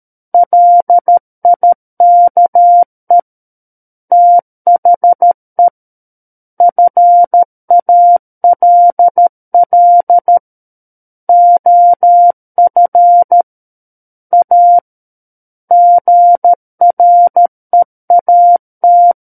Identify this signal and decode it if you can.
Morse code, heard as a keyed tone, about 13 wpm, 700 Hz. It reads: LIKE THE FALL OF A GREAT
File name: Book One / 04-Chapter 4 - 13 WPM